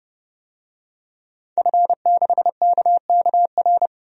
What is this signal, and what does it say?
Morse code, heard as a keyed tone, about 32 wpm, 700 Hz. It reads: F6KKR
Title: DXCC-100 Call Signs - 5 Characters - 30wpm